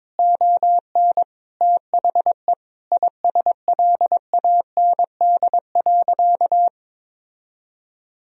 Morse code, heard as a keyed tone, about 21 wpm, 700 Hz. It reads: ON THE ISLAND.
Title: Sentences from Top 1000 Words - Review Part 3 - 22wpm